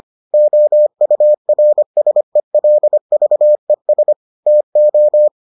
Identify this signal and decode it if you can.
Morse code, heard as a keyed tone, about 25 wpm, 600 Hz. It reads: OURSELVES TO